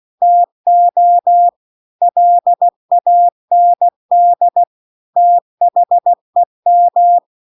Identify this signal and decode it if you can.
Morse code, heard as a keyed tone, about 16 wpm, 700 Hz. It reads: TO LAND THEM
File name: CW-The-Big-Twoheaded-River-2-16WPM